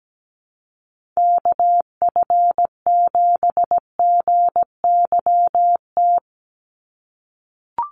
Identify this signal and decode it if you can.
Morse code, heard as a keyed tone, about 17 wpm, 700 Hz. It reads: KF7GYT
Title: USA Call Signs 17wpm